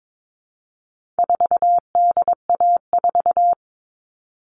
Morse code, heard as a keyed tone, about 22 wpm, 700 Hz. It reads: 4DA4
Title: All Cut Numbers - 4-Digit 22wpm